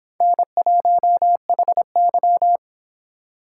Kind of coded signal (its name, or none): Morse code